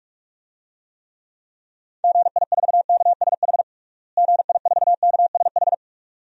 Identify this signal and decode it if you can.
Morse code, heard as a keyed tone, about 46 wpm, 700 Hz. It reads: GI4KSH GI4KSH
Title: DXCC-100 Call Signs - 6 Characters - Repeated 2x - 45wpm